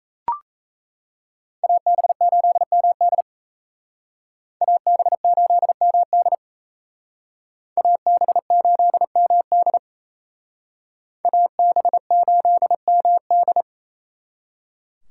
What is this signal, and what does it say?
Morse code, heard as a keyed tone, about 34 wpm, 700 Hz. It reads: AB8MD AB8MD AB8MD AB8MD
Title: Speed Racing - USA 2x2 Call Signs 28wpm